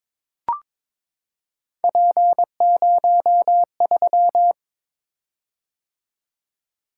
Morse code, heard as a keyed tone, about 22 wpm, 700 Hz. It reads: P03
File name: practice / 22wpm / Three Letter-Number Combinations - 22wpm